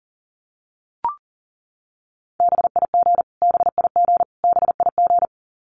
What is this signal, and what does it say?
Morse code, heard as a keyed tone, about 40 wpm, 700 Hz. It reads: BIG BIG BIG
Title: Top 500 Words Repeated 40wpm